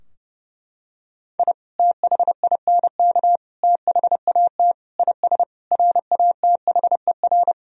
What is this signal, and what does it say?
Morse code, heard as a keyed tone, about 32 wpm, 700 Hz. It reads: I THINK THAT IS RATHER
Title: Sentences from Top 1000 Words - Review Part 4 - 30wpm